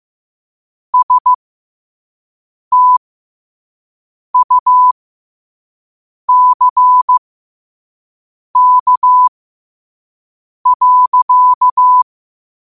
Morse code, heard as a keyed tone, about 15 wpm, spaced as for 5 wpm, 1000 Hz. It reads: STUCK.